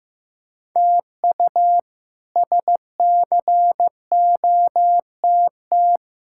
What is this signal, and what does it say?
Morse code, heard as a keyed tone, about 15 wpm, 700 Hz. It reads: TU SCOTT
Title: K1USN SST QSO Practice - 15wpm